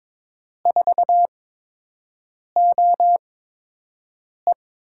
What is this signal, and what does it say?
Morse code, heard as a keyed tone, about 22 wpm, spaced as for 6 wpm, 700 Hz. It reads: 4OE
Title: USA Call Signs - 5 Characters - 8x Character Spacing - 22wpm